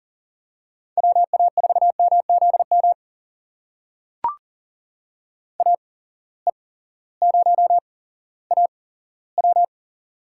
Morse code, heard as a keyed tone, about 40 wpm, spaced as for 18 wpm, 700 Hz. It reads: WA4MZM AE0AW
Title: USA Call Signs - 8x Character Spacing - 40wpm